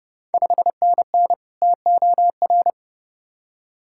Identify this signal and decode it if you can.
Morse code, heard as a keyed tone, about 32 wpm, 700 Hz. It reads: 5NN TOR
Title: Ontario QSO Party Exchange 30wpm